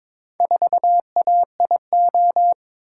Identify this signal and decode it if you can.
Morse code, heard as a keyed tone, about 22 wpm, 700 Hz. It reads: 4AIO
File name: practice / 22wpm / Four Letter-Number Combinations - 22wpm